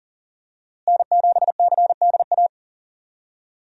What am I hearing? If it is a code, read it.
Morse code, heard as a keyed tone, about 40 wpm, 700 Hz. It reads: N7CDA
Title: USA Call Signs - 5 Characters - Repeated 2x - 40wpm